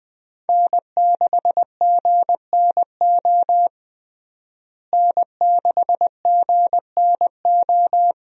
Morse code, heard as a keyed tone, about 20 wpm, 700 Hz. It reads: N6GNO N6GNO
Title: USA Call Signs - 5 Characters - Repeated 3x - 20wpm